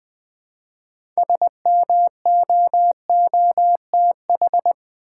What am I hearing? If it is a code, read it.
Morse code, heard as a keyed tone, about 20 wpm, 700 Hz. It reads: SMOOTH
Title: Six Letter Words - 20wpm